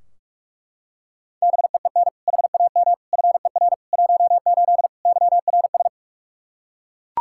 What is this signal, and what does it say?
Morse code, heard as a keyed tone, about 41 wpm, 700 Hz. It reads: BEEN HAM FER 18 YRS
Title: Practice QSO - Exchange 3 - More Info - 45wpm